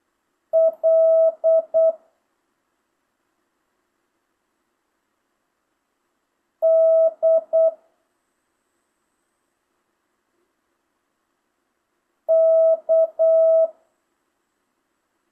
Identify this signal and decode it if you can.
Morse code, heard as a keyed tone, about 8 wpm, 650 Hz. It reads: L D K
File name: jh8cbh / 2025cw